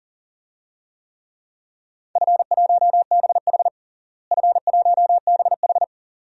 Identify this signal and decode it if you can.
Morse code, heard as a keyed tone, about 40 wpm, 700 Hz. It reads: F1BH F1BH
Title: DXCC-100 Call Signs - 4 Characters - Repeated 2x - 40wpm